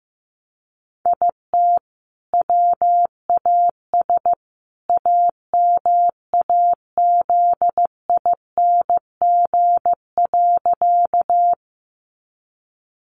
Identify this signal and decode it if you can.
Morse code, heard as a keyed tone, about 15 wpm, 700 Hz. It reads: IT WAS AMAZING.
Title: Sentences from Top 1500 Words 15wpm